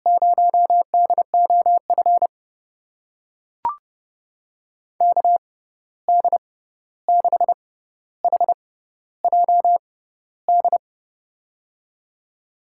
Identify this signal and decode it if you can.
Morse code, heard as a keyed tone, about 30 wpm, spaced as for 14 wpm, 700 Hz. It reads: 0DOF KD6HJD
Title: USA Call Signs - 6 Characters - 6x Character Spacing - 30wpm